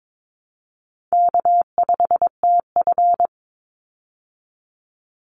Morse code, heard as a keyed tone, about 22 wpm, 700 Hz. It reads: K5TF
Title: Call Signs Repeated 2x - 22wpm